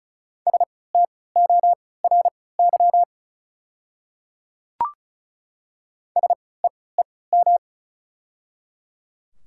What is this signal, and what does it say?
Morse code, heard as a keyed tone, about 36 wpm, spaced as for 20 wpm, 700 Hz. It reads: STORY SEEM
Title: 3x Character Spacing - Top 300 Words - 35wpm